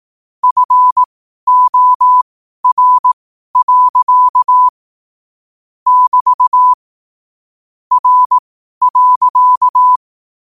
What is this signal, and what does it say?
Morse code, heard as a keyed tone, about 18 wpm, spaced as for 12 wpm, 1000 Hz. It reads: FOR. <BT> R.
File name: qotdfiles / 12wpm